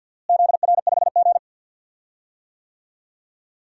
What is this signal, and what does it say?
Morse code, heard as a keyed tone, about 50 wpm, 700 Hz. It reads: ZR5G
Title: DXCC-100 Call Signs - 4 Characters - Repeated 3x - 50wpm